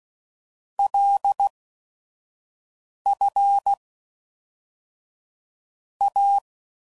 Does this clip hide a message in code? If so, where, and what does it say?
Yes: Morse code from 0.8 to 6.4 s, heard as a keyed tone, about 16 wpm, 800 Hz. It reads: L F A